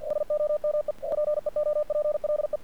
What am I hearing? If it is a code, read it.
Morse code, heard as a keyed tone, about 48 wpm, 600 Hz. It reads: SOME PEOPLE
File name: archived / CRSnet